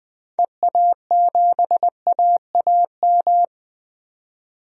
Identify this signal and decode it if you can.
Morse code, heard as a keyed tone, about 20 wpm, 700 Hz. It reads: EA7AAM
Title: DXCC-100 Call Signs - 6 Characters - 20wpm